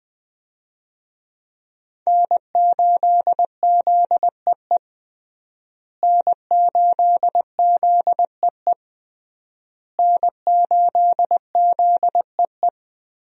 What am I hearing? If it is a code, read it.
Morse code, heard as a keyed tone, about 20 wpm, 700 Hz. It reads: N8ZEE N8ZEE N8ZEE
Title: USA Call Signs - 5 Characters - Repeated 3x - 20wpm